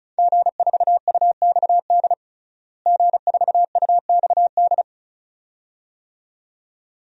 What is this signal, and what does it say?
Morse code, heard as a keyed tone, about 36 wpm, 700 Hz. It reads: G4UXD G4UXD
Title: DXCC-100 Call Signs - 5 Characters - Repeated 2x - 35wpm